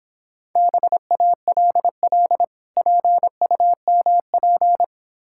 Morse code, heard as a keyed tone, about 26 wpm, 690 Hz. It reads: BALL PUMP